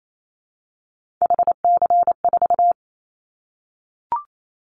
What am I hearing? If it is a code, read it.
Morse code, heard as a keyed tone, about 28 wpm, 700 Hz. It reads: HC4
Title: Callsign Prefixes for Top 100 DXCC - 28wpm